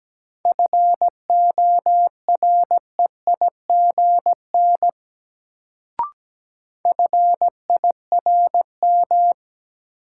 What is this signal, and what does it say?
Morse code, heard as a keyed tone, about 17 wpm, 700 Hz. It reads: FOREIGN FIRM
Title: Sentences from Top 1400 Words - New Word Drill Alt - 17wpm